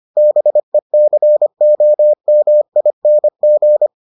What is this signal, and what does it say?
Morse code, heard as a keyed tone, about 25 wpm, 600 Hz. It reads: BECOMING